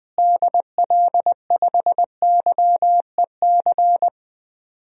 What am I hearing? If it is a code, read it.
Morse code, heard as a keyed tone, about 20 wpm, 700 Hz. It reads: DL5YEC
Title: DXCC-100 Call Signs - 6 Characters - Repeated 2x - 20wpm